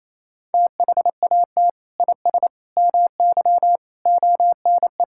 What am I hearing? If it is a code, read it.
Morse code, heard as a keyed tone, about 28 wpm, 700 Hz. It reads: THAT IS MY ONE
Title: Sentences from Top 1000 Words - Review Part 5 - 28wpm